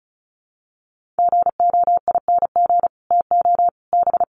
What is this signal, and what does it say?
Morse code, heard as a keyed tone, about 34 wpm, 700 Hz. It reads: GOING TO B
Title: Sets of 5 Words Encore2 35wpm